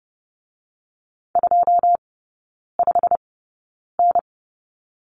Morse code, heard as a keyed tone, about 30 wpm, spaced as for 9 wpm, 700 Hz. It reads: 25N
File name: course / 30:10wpm / 087 Lesson 15e - Number 5 - Three Characters 30wpm